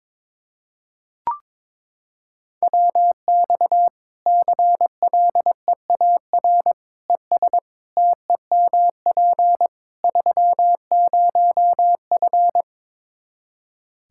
Morse code, heard as a keyed tone, about 22 wpm, 700 Hz. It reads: WX CLEAR ES TEMP 30F
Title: Practice QSO - Exchange 2 - Info - 22wpm